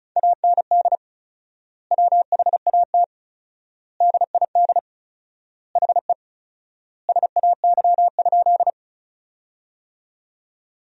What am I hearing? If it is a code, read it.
Morse code, heard as a keyed tone, about 36 wpm, 700 Hz. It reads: AND WHAT DID HE SAY?
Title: 4x Word-Spacing - Sentences from Top 200 Words - 35wpm